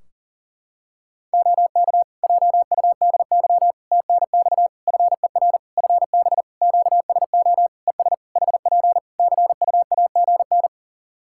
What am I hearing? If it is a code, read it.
Morse code, heard as a keyed tone, about 40 wpm, 700 Hz. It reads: OK JUDY TNX FER FB QSO ES HP CUAGN
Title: Practice QSO - Exchange 4 - Ending - 40wpm